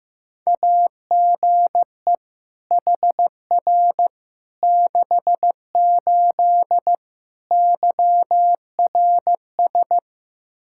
Morse code, heard as a keyed tone, about 15 wpm, 700 Hz. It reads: AGE HR 68 YRS
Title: Practice QSO - Exchange 3 - More Info - 15wpm